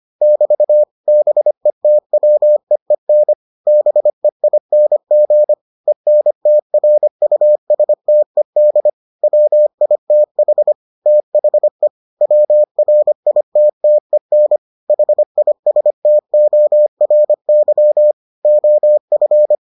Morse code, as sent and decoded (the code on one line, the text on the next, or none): -...- / -... - .-- . -. / -... .. -. --. / . -. - .-. ..- ... - . -.. / .-- .. - .... / - .... / .-- .-. .. - - . -. / .... .. ... - --- .-. -.-- / --- ..-.
<BT> BETWEEN BEING ENTRUSTED WITH THE WRITTEN HISTORY OF